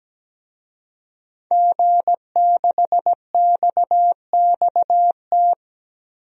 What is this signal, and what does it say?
Morse code, heard as a keyed tone, about 17 wpm, 700 Hz. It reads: G6XXT
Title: DXCC-100 Call Signs - 5 Characters - 17wpm